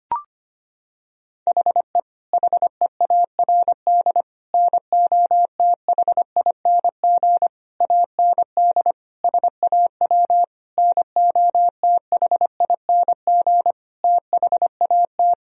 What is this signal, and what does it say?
Morse code, heard as a keyed tone, about 25 wpm, 700 Hz. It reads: HE HEARD NOTHING AND SAW NOTHING THAT
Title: Sentences from Top 1000 Words - Review Part 3 - 25wpm